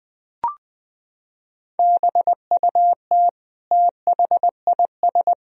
Morse code, heard as a keyed tone, about 20 wpm, 700 Hz. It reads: BUT THIS
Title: Sentences from Top 1000 Words - Review Part 2 - 20wpm